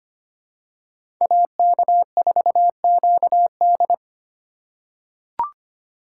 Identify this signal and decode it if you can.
Morse code, heard as a keyed tone, about 25 wpm, 700 Hz. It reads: AK4QD
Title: USA Call Signs - 5 Characters - 3x Character Spacing - 25wpm